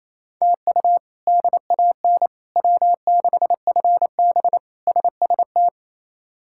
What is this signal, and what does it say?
Morse code, heard as a keyed tone, about 28 wpm, 700 Hz. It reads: TU DAN W6FB SST
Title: K1USN SST QSO Practice - 28wpm